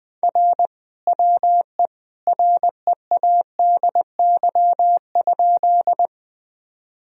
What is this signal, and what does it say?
Morse code, heard as a keyed tone, about 21 wpm, 700 Hz. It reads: R WE READY?
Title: Sentences from Top 400 Words and Abbreviations 20wpm